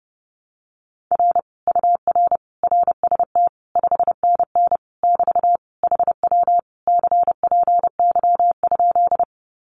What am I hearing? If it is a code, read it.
Morse code, heard as a keyed tone, about 30 wpm, 700 Hz. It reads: R UR RST 5NN <BT> HW CPY?
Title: Common QSO Phrases Encore 30wpm